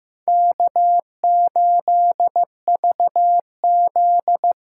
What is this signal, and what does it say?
Morse code, heard as a keyed tone, about 15 wpm, 700 Hz. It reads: K8VZ
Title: USA Call Signs - 4 Characters - 4x Character Spacing - 15wpm